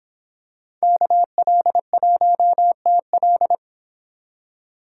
Morse code, heard as a keyed tone, about 26 wpm, 700 Hz. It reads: KL1TL